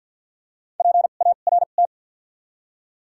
Morse code, heard as a keyed tone, about 54 wpm, 700 Hz. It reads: P A R T
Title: 2x Character Spacing - Top 300 Words - 50wpm